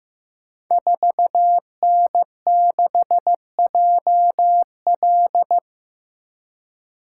Morse code, heard as a keyed tone, about 15 wpm, 700 Hz. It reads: 4N6JL
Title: Code Groups - 5 Letter:Numbers - 15wpm